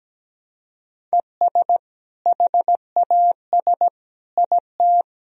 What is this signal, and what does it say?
Morse code, heard as a keyed tone, about 17 wpm, 700 Hz. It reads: ES HAS IT
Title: Sentences from Top 100 Words and Abbreviations 17wpm